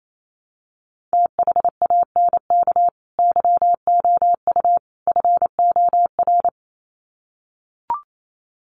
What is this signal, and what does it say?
Morse code, heard as a keyed tone, about 28 wpm, 700 Hz. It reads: THANK YOU FOR
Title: Sets of 3 Words 28wpm